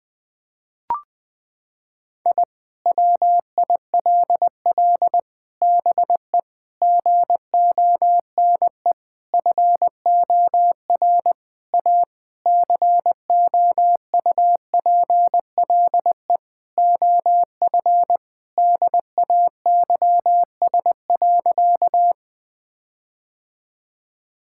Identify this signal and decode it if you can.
Morse code, heard as a keyed tone, about 21 wpm, 700 Hz. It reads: I WILL BE GONE FOR A COUPLE OF DAYS.
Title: Sentences with 4-Word Phrases - 20wpm